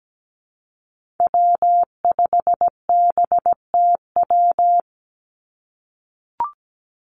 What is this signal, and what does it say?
Morse code, heard as a keyed tone, about 17 wpm, 700 Hz. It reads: W5BTW
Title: USA Call Signs 17wpm